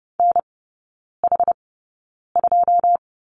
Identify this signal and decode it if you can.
Morse code, heard as a keyed tone, about 30 wpm, spaced as for 9 wpm, 700 Hz. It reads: NH2
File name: Lesson 19e - Letter w - Three Characters 30wpm